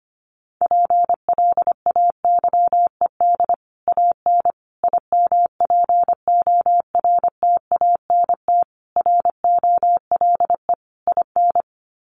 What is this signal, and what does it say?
Morse code, heard as a keyed tone, about 25 wpm, 700 Hz. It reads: PLAYED AN IMPORTANT ROLE IN
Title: Sets of 5 Words Encore2 25wpm